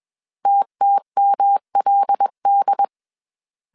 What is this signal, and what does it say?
Morse code, heard as a keyed tone, about 21 wpm, 800 Hz. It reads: TTMLD
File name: CW / DET / 1 6 1500C S A 20wpm 2048